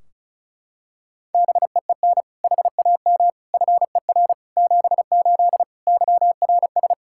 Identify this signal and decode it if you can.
Morse code, heard as a keyed tone, about 36 wpm, 700 Hz. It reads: BEEN HAM FER 78 YRS
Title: Practice QSO - Exchange 3 - More Info - 35wpm